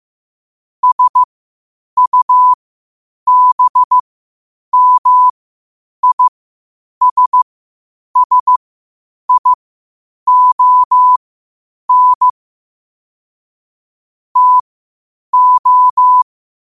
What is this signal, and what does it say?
Morse code, heard as a keyed tone, about 15 wpm, spaced as for 8 wpm, 1000 Hz. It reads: SUBMISSION TO
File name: qotdfiles / 8wpm